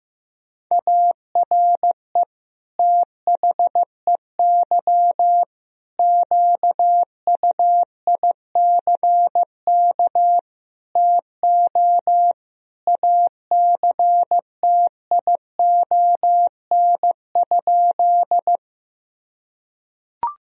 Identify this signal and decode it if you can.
Morse code, heard as a keyed tone, about 15 wpm, 700 Hz. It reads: ARE THEY QUICK TO ACTION?
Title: Sentences from Top 1000 Words - Review Part 4 - 15wpm